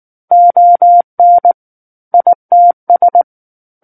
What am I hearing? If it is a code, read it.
Morse code, heard as a keyed tone, about 19 wpm, 700 Hz. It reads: ON ITS
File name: Book One / 10-Chapter 10 - 19 WPM